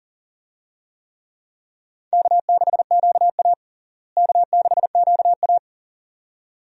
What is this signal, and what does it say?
Morse code, heard as a keyed tone, about 40 wpm, 700 Hz. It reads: K6QA K6QA
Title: USA Call Signs - 4 Characters - Repeated 2x - 40wpm